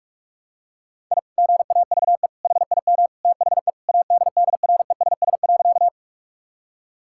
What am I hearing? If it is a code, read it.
Morse code, heard as a keyed tone, about 41 wpm, 700 Hz. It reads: I GAVE HIM THE ADDRESS.
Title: Sentences from Top 1000 Words - Review Part 6 - 45wpm